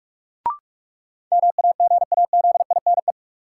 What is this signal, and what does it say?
Morse code, heard as a keyed tone, about 44 wpm, 700 Hz. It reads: MAGAZINE
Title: Sentences from Top 1000 Words - Review Part 7 - New Word Drill - 45wpm